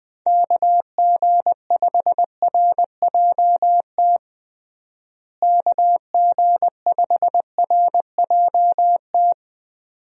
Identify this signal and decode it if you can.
Morse code, heard as a keyed tone, about 20 wpm, 700 Hz. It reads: KG5RJT KG5RJT
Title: USA Call Signs - 6 Characters - Repeated 2x - 20wpm